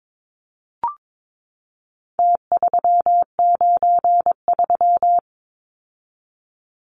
Morse code, heard as a keyed tone, about 22 wpm, 700 Hz. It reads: T393
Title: Numbers - Contest Style - 4-Digit 22wpm